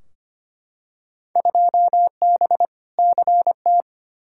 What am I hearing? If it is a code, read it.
Morse code, heard as a keyed tone, about 25 wpm, 700 Hz. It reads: 2B CT
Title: Field Day Exchanges Encore - 25wpm